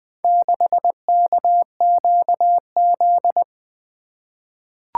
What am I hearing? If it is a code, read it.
Morse code, heard as a keyed tone, about 20 wpm, 700 Hz. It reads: 6KQZ
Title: Four Letter-Number Combinations - 20wpm